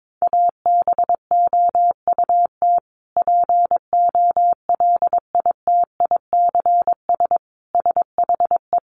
Morse code, heard as a keyed tone, about 22 wpm, 700 Hz. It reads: ABOUT POLITICS SHE
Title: Sentences from Top 1000 Words 22wpm